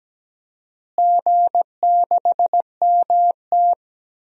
Morse code, heard as a keyed tone, about 17 wpm, 700 Hz. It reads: G6MT
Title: DXCC-100 Call Signs - 4 Characters - Repeated 2x - 17wpm